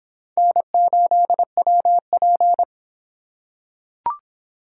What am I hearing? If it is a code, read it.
Morse code, heard as a keyed tone, about 26 wpm, 700 Hz. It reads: N8WP